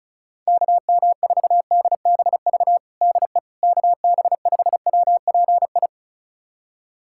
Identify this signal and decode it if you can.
Morse code, heard as a keyed tone, about 35 wpm, 700 Hz. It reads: KM4DBV DE KB5WPI
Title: USA Callsign DE Callsign - 6 Characters - 35wpm